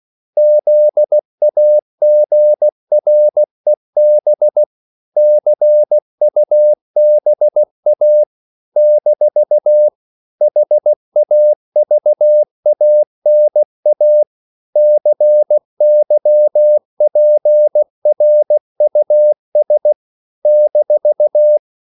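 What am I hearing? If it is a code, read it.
Morse code, heard as a keyed tone, about 16 wpm, 600 Hz. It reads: ZAGREB CUBA - HAVANA CYPRUS -
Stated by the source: CW practice of national capitals at 16 words per minute, from ON6ZQ.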